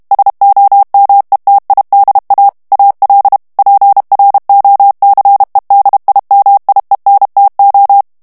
Morse code, heard as a keyed tone, about 32 wpm, 800 Hz. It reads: SOMETIDA AL PROCEDIMIENTO
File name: ESPAGNOL TEXTE EN CLAIR 2 VITESSE 30 MOTS MINUTES